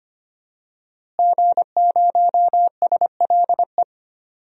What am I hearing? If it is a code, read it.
Morse code, heard as a keyed tone, about 25 wpm, 700 Hz. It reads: G0SLE
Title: DXCC-100 Call Signs - 5 Characters - Repeated 3x - 25wpm